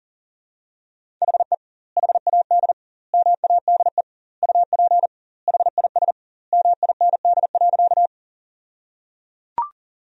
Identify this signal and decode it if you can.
Morse code, heard as a keyed tone, about 37 wpm, 700 Hz. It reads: HE HAD MADE UP HIS MIND.
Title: 2x Word-Spacing - Sentences from Top 500 Words - 40wpm